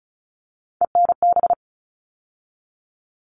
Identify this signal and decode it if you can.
Morse code, heard as a keyed tone, about 34 wpm, 700 Hz. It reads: ENB
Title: All Cut Numbers - 3-Digit 35wpm